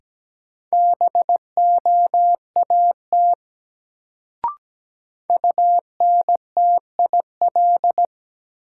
Morse code, heard as a keyed tone, about 17 wpm, 700 Hz. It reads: BOAT UNTIL
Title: Sentences from Top 500 Words - New Word Drill - 17wpm